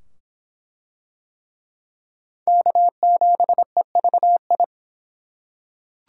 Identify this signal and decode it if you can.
Morse code, heard as a keyed tone, about 24 wpm, 700 Hz. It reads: K7EVI